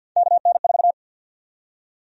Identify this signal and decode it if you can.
Morse code, heard as a keyed tone, about 53 wpm, 700 Hz. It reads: KN4
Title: USA Call Sign Prefixes - 50wpm